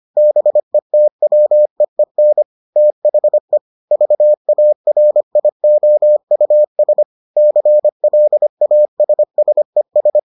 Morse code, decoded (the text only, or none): BETWEEN THE VARIOUS CLASSES